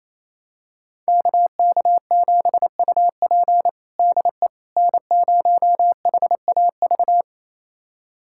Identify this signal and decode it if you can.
Morse code, heard as a keyed tone, about 27 wpm, 700 Hz. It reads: KK7UP DE N0HAV
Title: USA Callsign DE Callsign - 5 Characters - 28wpm